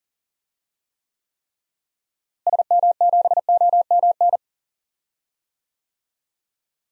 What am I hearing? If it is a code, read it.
Morse code, heard as a keyed tone, about 40 wpm, 700 Hz. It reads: SM7OMN
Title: DXCC-100 Call Signs - 6 Characters - Repeated 3x - 40wpm